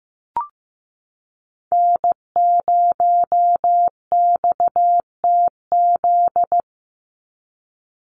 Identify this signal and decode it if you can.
Morse code, heard as a keyed tone, about 15 wpm, 700 Hz. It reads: N0XTZ
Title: USA Call Signs 15wpm